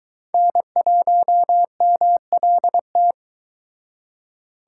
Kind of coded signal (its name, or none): Morse code